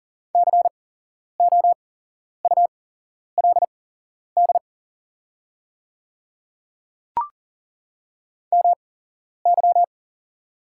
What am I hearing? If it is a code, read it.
Morse code, heard as a keyed tone, about 40 wpm, spaced as for 11 wpm, 700 Hz. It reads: COULD MY
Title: 8x Character Spacing - Top 100 Words - 40wpm